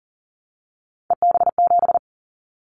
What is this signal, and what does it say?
Morse code, heard as a keyed tone, about 40 wpm, 700 Hz. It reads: EB7
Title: Callsign Prefixes for Top 25 DXCC - 40wpm